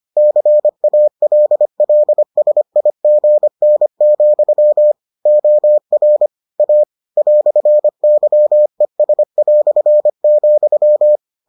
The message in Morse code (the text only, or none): CALLSIGN, OR A "YES",